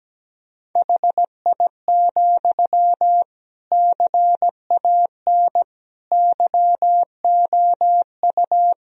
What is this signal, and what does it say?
Morse code, heard as a keyed tone, about 17 wpm, 700 Hz. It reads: HI, CAN YOU
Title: Sentences from Top 1000 Words - Review Part 3 - 17wpm